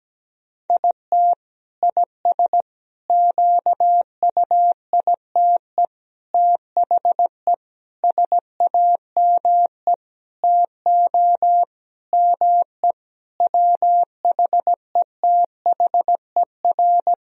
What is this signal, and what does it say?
Morse code, heard as a keyed tone, about 17 wpm, 700 Hz. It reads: IT IS QUITE THE SAME TO ME WHETHER
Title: Sentences from Top 1000 Words - Review Part 3 - 17wpm